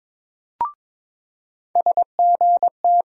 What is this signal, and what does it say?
Morse code, heard as a keyed tone, about 22 wpm, 700 Hz. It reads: SGT
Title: USA Call Sign Postfixes - 3 Letters 22wpm